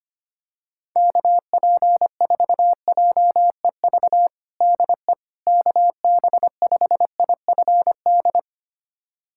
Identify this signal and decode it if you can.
Morse code, heard as a keyed tone, about 25 wpm, 700 Hz. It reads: KP4JEV DE KB5IFD
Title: USA Callsign DE Callsign 25wpm